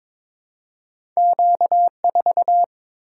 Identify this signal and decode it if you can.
Morse code, heard as a keyed tone, about 22 wpm, 700 Hz. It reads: Q4
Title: Single-Letter Single-Number 22wpm